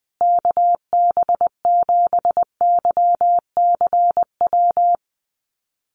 Morse code, heard as a keyed tone, about 20 wpm, 700 Hz. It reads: KB7YCW
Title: USA Call Signs Repeated 2x 20wpm